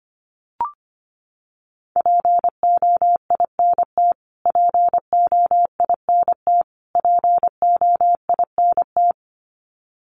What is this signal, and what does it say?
Morse code, heard as a keyed tone, about 25 wpm, 700 Hz. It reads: POINT POINT POINT
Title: Top 200 Words Repeated 25wpm